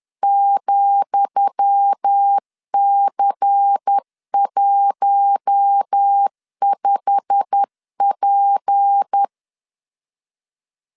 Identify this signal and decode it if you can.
Morse code, heard as a keyed tone, about 10 wpm, 800 Hz. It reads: ,C15P
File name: LEZIONI / 10 SIGNS / 10D COMA 10wpm 3013